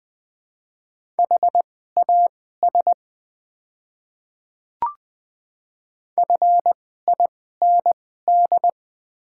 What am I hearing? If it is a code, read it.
Morse code, heard as a keyed tone, about 20 wpm, spaced as for 14 wpm, 690 Hz. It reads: HAS FIND
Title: 2x Character Spacing - Top 100 Words - 20wpm